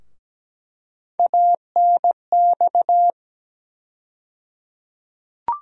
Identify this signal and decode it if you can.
Morse code, heard as a keyed tone, about 17 wpm, 700 Hz. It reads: ANX
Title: USA Call Sign Postfixes - 3 Letters 17wpm